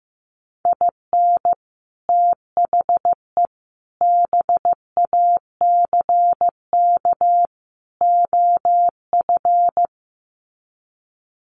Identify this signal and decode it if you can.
Morse code, heard as a keyed tone, about 15 wpm, 700 Hz. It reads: IN THE BACK OF
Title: Sets of 4 Words 15wpm